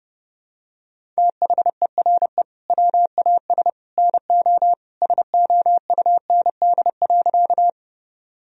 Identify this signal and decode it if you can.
Morse code, heard as a keyed tone, about 30 wpm, 700 Hz. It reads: THERE WAS NO SOUND.
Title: Sentences from Top 1000 Words - Review Part 1 - 30wpm